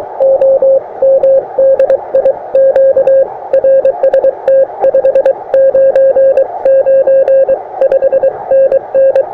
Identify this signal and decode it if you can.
Morse code, heard as a keyed tone, about 22 wpm, 530 Hz. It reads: OM DI Q RST 599 5NN
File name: attachments / europe 20m